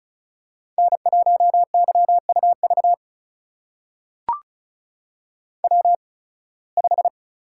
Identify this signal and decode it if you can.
Morse code, heard as a keyed tone, about 36 wpm, spaced as for 23 wpm, 700 Hz. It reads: N1YUV W5
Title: USA Call Signs - 5 Characters - 8x Character Spacing - 35wpm